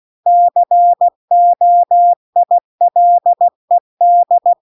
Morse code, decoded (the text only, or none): COILED